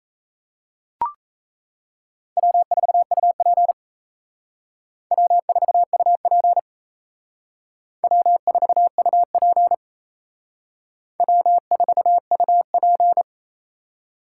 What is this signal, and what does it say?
Morse code, heard as a keyed tone, about 35 wpm, 700 Hz. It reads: W4UP W4UP W4UP W4UP
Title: Speed Racing - USA 1x2 Call Signs 28wpm